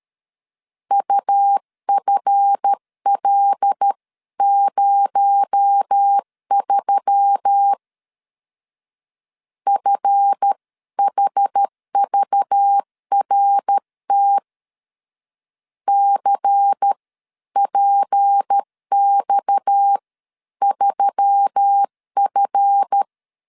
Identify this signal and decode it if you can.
Morse code, heard as a keyed tone, about 13 wpm, spaced as for 11 wpm, 800 Hz. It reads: UFL03 FHVRT CPX3F
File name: LEZIONI / 7 GF9Z / 7 GF 12wpm 2122